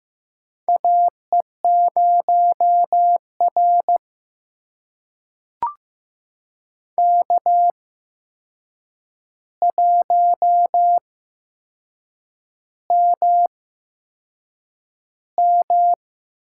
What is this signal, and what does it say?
Morse code, heard as a keyed tone, about 15 wpm, spaced as for 6 wpm, 700 Hz. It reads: AE0R K1MM
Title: USA Call Signs - 4 Characters - 8x Character Spacing - 15wpm